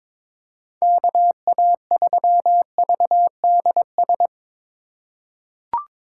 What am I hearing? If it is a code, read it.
Morse code, heard as a keyed tone, about 22 wpm, 700 Hz. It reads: KA3VDS
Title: USA Call Signs - 6 Characters - 4x Character Spacing - 22wpm